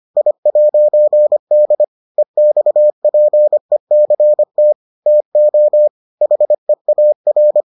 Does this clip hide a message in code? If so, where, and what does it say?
Yes: Morse code from 0.2 to 7.6 s, heard as a keyed tone, about 25 wpm, 600 Hz. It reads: I'D EXPECT TO HEAR